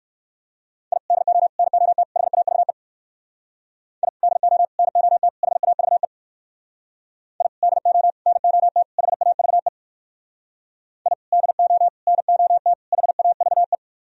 Morse code, heard as a keyed tone, about 59 wpm, 700 Hz. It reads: I DO NOT HAVE I DO NOT HAVE I DO NOT HAVE I DO NOT HAVE